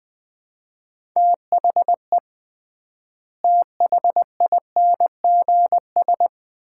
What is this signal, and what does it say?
Morse code, heard as a keyed tone, about 20 wpm, 700 Hz. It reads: THE THINGS
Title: 3x Word-Spacing - Sets of 3 Words - 20wpm